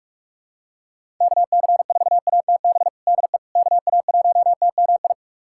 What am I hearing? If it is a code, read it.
Morse code, heard as a keyed tone, about 46 wpm, 700 Hz. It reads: KC4ATB DE KA1TMI
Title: USA Callsign DE Callsign - 6 Characters - 45wpm